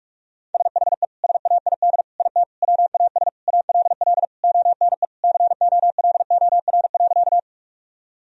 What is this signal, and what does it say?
Morse code, heard as a keyed tone, about 44 wpm, 700 Hz. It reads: SHE SAID IT WAS ALL ONE COLOR.
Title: Sentences from Top 1000 Words - Review Part 2 - 45wpm